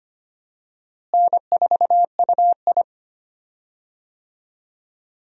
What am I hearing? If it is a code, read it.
Morse code, heard as a keyed tone, about 25 wpm, 700 Hz. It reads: N4UI
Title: USA Call Signs - 4 Characters - Repeated 2x - 25wpm